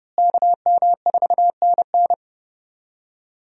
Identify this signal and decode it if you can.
Morse code, heard as a keyed tone, about 30 wpm, 700 Hz. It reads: KM4NN
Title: USA Call Signs - 5 Characters - Repeated 2x - 30wpm